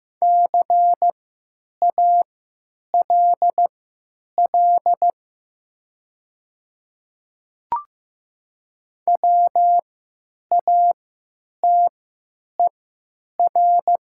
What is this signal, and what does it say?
Morse code, heard as a keyed tone, about 15 wpm, spaced as for 9 wpm, 700 Hz. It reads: CALL WATER
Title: 3x Character Spacing - Top 100 Words - 15wpm